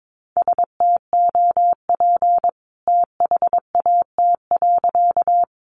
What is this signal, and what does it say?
Morse code, heard as a keyed tone, about 22 wpm, 700 Hz. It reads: STOP THAT.
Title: A Sound Of Thunder 22wpm